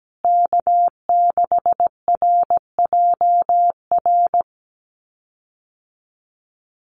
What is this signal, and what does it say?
Morse code, heard as a keyed tone, about 17 wpm, 700 Hz. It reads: K6RJR
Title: USA Call Signs Repeated 2x 17wpm